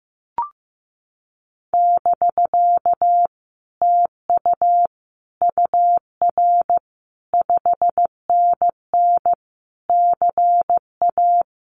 Morse code, heard as a keyed tone, about 15 wpm, 700 Hz. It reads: <BK> TU UR 5NN CA
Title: Common QSO Phrases Encore 15wpm